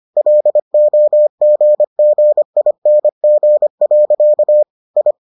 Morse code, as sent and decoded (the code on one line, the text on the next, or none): .-.. --- --. --. .. -. --. .-.-.- / ..
LOGGING. I